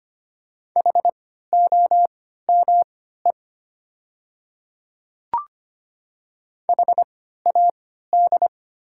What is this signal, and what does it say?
Morse code, heard as a keyed tone, about 25 wpm, spaced as for 14 wpm, 700 Hz. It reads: HOME HAD
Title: 3x Character Spacing - Top 500 Words - 25wpm